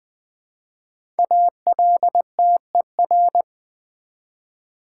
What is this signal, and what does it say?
Morse code, heard as a keyed tone, about 20 wpm, 700 Hz. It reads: ALTER
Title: Build A Word - 1 Letter At A Time - 3 to 5-Letter Words - 20wpm